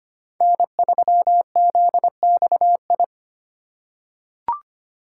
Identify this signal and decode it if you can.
Morse code, heard as a keyed tone, about 25 wpm, 700 Hz. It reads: N3ZXI
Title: USA Call Signs - 5 Characters - 2x Character Spacing - 25wpm